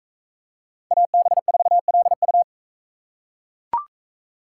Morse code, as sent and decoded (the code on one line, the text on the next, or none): .- -... ....- .-.. ..-
AB4LU